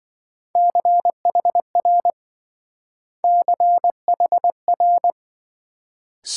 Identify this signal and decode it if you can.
Morse code, heard as a keyed tone, about 26 wpm, 700 Hz. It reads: CHR CHR